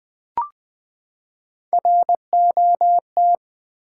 Morse code, heard as a keyed tone, about 20 wpm, 700 Hz. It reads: ROT
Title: Build A Word - 3 Letters At A Time - 6-Letter Words - 20wpm